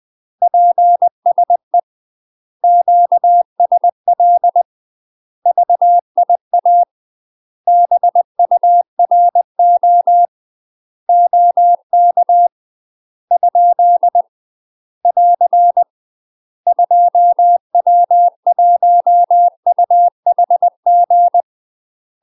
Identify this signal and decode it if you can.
Morse code, heard as a keyed tone, about 20 wpm, 700 Hz. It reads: PSE QSL VIA BURO OK ? <AR> 2W1UHG
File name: UK MASTER QSOs TRAINING / 20WPM / UK MASTER QSOs 0069